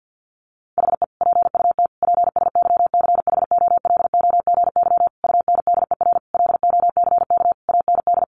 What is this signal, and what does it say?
Morse code, heard as a keyed tone, about 48 wpm, 700 Hz. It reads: HE PUT PSYCHOLOGY UNDER LOCK AND
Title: Sentences from Top 1400 Words 50wpm